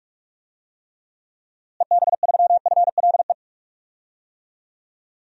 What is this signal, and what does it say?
Morse code, heard as a keyed tone, about 42 wpm, 700 Hz. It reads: EB3FLE
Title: DXCC-100 Call Signs - 6 Characters - Repeated 2x - 45wpm